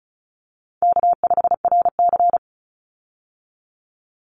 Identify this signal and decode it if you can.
Morse code, heard as a keyed tone, about 34 wpm, 700 Hz. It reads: K5RC
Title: Call Signs Repeated 3x - 35wpm